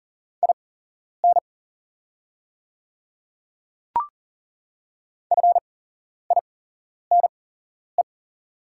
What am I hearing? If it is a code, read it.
Morse code, heard as a keyed tone, about 40 wpm, spaced as for 11 wpm, 690 Hz. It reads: IN FINE
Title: 8x Character Spacing - Top 500 Words - 40wpm